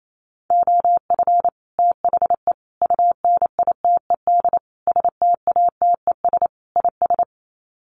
Morse code, heard as a keyed tone, about 28 wpm, 700 Hz. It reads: OF THE UNITED STATES IS
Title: Sets of 5 Words Encore2 28wpm